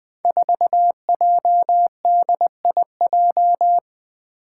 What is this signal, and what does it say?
Morse code, heard as a keyed tone, about 20 wpm, 700 Hz. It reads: 4JDIJ
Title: Five Challenging Letter-Number Combinations 20wpm